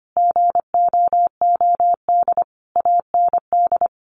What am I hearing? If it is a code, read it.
Morse code, heard as a keyed tone, about 25 wpm, 700 Hz. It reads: GOOD AND
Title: Sentences from Top 600 Words 25wpm